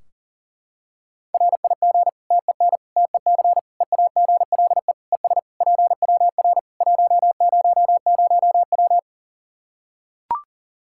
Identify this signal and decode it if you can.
Morse code, heard as a keyed tone, about 40 wpm, 700 Hz. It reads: RIG TEN TEC EAGLE ES PWR 100W
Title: Practice QSO - Exchange 2 - Info - 40wpm